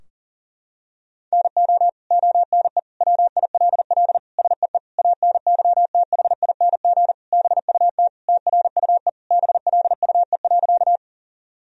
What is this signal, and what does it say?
Morse code, heard as a keyed tone, about 40 wpm, 700 Hz. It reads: NO ONE WILL SEE ANYTHING BUT TRUE BLUE.
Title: Sentences from Top 1000 Words - Review Part 3 - 40wpm